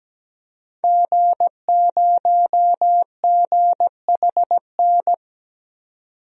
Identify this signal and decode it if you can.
Morse code, heard as a keyed tone, about 17 wpm, 700 Hz. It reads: G0GHN
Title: DXCC-100 Call Signs - 5 Characters - 17wpm